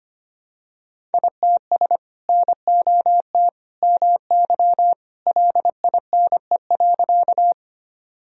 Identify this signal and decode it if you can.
Morse code, heard as a keyed tone, about 25 wpm, 700 Hz. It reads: ITS NOT MY LINE.
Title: Sentences from Top 1000 Words - Review Part 1 - 25wpm